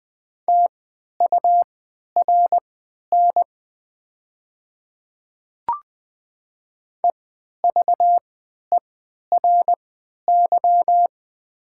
Morse code, heard as a keyed tone, about 20 wpm, spaced as for 11 wpm, 700 Hz. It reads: TURN EVERY
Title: 3x Character Spacing - Top 200 Words - 20wpm